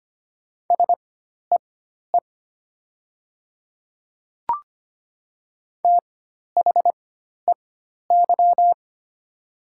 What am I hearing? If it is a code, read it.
Morse code, heard as a keyed tone, about 25 wpm, spaced as for 12 wpm, 700 Hz. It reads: SEE THEY
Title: 4x Character Spacing - Top 100 Words - 25wpm